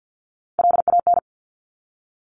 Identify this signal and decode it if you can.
Morse code, heard as a keyed tone, about 50 wpm, 700 Hz. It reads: LAN
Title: Top Three Letter Combinations in Words 50wpm